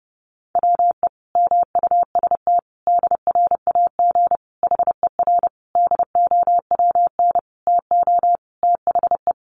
Morse code, heard as a keyed tone, about 30 wpm, 700 Hz. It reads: WE MUST DRAG HER DOWN TO THE
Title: Sentences from Top 1500 Words 30wpm